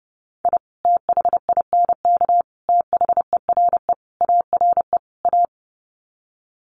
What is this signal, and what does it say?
Morse code, heard as a keyed tone, about 30 wpm, 700 Hz. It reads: I THINK THERE ARE A
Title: Sets of 5 Words Encore2 30wpm